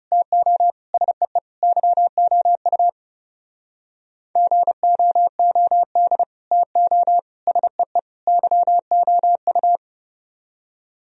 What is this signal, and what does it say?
Morse code, heard as a keyed tone, about 37 wpm, 700 Hz. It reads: TO SEE YOU GOOD TO SEE YOU
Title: Speed-Racing - Sets of 4 Words 30wpm